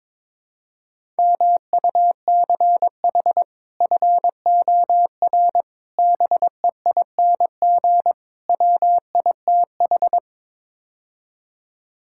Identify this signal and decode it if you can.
Morse code, heard as a keyed tone, about 22 wpm, 700 Hz. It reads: MUCH FOR BEING WITH
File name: practice / 22wpm / Build A Phrase - 5-Word Phrases - 22wpm